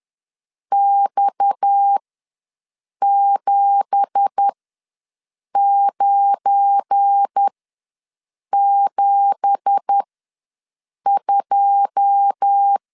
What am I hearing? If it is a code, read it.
Morse code, heard as a keyed tone, about 10 wpm, spaced as for 6 wpm, 800 Hz. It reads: X7972
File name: LEZIONI / 9 2867 ws 10wpm 4215